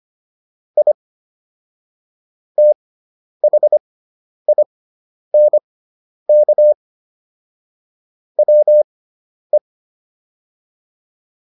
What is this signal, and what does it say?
Morse code, heard as a keyed tone, about 25 wpm, spaced as for 10 wpm, 600 Hz. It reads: I THINK WE